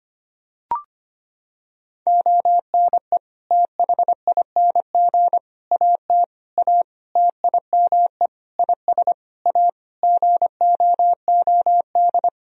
Morse code, heard as a keyed tone, about 25 wpm, 700 Hz. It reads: ONE THING AT A TIME IS A GOOD
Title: Sentences from Top 1000 Words - Review Part 2 - 25wpm